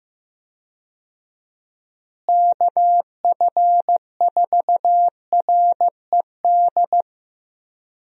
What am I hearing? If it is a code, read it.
Morse code, heard as a keyed tone, about 15 wpm, 700 Hz. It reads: KF4RED
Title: USA Call Signs - 6 Characters - Repeated 3x - 15wpm